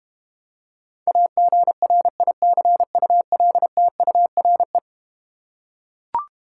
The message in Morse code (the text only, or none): AGRICULTURE